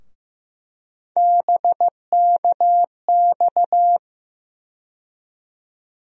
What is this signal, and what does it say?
Morse code, heard as a keyed tone, about 15 wpm, 700 Hz. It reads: BKX
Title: USA Call Sign Postfixes - 3 Letters 15wpm